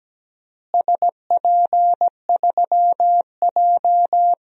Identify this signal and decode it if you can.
Morse code, heard as a keyed tone, about 17 wpm, 700 Hz. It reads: SP3J
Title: DXCC-100 Call Signs - 4 Characters - Repeated 3x - 17wpm